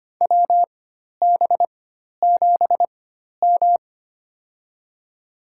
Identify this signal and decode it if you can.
Morse code, heard as a keyed tone, about 25 wpm, spaced as for 12 wpm, 700 Hz. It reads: WB7M
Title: USA Call Signs - 4 Characters - 4x Character Spacing - 25wpm